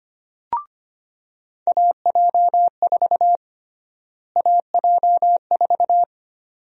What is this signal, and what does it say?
Morse code, heard as a keyed tone, about 25 wpm, 700 Hz. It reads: AJ4 AJ4
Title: USA Call Sign Prefixes - Repeated 2x - 25wpm